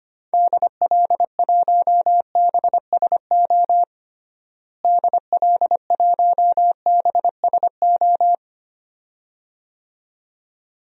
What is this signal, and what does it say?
Morse code, heard as a keyed tone, about 25 wpm, 700 Hz. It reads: DL1BSO DL1BSO
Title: DXCC-100 Call Signs - 6 Characters - Repeated 2x - 25wpm